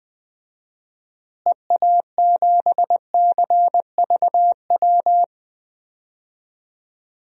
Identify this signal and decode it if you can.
Morse code, heard as a keyed tone, about 20 wpm, 700 Hz. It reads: EA7CVW
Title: DXCC-100 Call Signs - 6 Characters - Repeated 3x - 20wpm